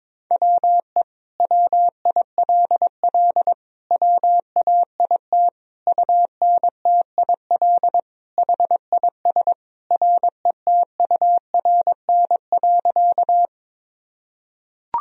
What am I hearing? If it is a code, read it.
Morse code, heard as a keyed tone, about 22 wpm, 700 Hz. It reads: WE WILL WAIT UNTIL HIS RETURN.
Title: Sentences from Top 1000 Words - Review Part 1 - 22wpm